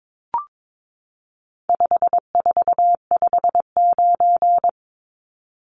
Morse code, heard as a keyed tone, about 22 wpm, 700 Hz. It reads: 5459
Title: Numbers - 4-Digit 22wpm